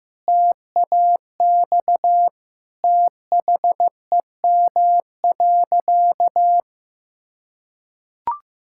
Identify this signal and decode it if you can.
Morse code, heard as a keyed tone, about 15 wpm, 700 Hz. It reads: TAX THEM.
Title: Sentences from Top 1000 Words - Review Part 5 - 15wpm